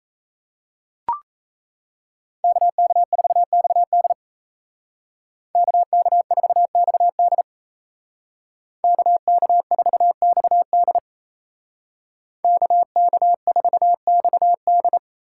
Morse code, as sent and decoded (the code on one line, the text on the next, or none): -.- -.- ....- -..- -.. / -.- -.- ....- -..- -.. / -.- -.- ....- -..- -.. / -.- -.- ....- -..- -..
KK4XD KK4XD KK4XD KK4XD